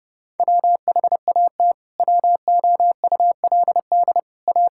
Morse code, heard as a keyed tone, about 30 wpm, 700 Hz. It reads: WHAT WOULD A
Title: Sentences from Top 1000 Words - Review Part 6 - 30wpm